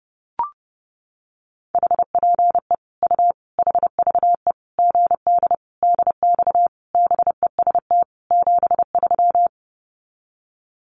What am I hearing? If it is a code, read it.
Morse code, heard as a keyed tone, about 30 wpm, 700 Hz. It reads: HPE U HVE GD DX BEST 73
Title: Common QSO Phrases 30wpm